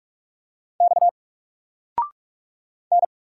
Morse code, heard as a keyed tone, about 44 wpm, 700 Hz. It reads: X N
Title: Single Letter-Number 45wpm